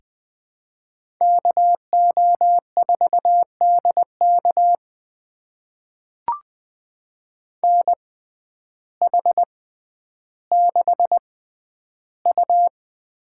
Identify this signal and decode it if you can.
Morse code, heard as a keyed tone, about 20 wpm, spaced as for 11 wpm, 700 Hz. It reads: KO4DK NH6U
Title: USA Call Signs - 6x Character Spacing - 20wpm